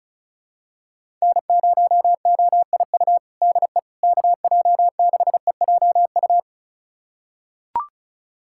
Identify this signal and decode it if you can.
Morse code, heard as a keyed tone, about 34 wpm, 700 Hz. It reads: N0OIU DE KJ6EJU
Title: USA Callsign DE Callsign 35wpm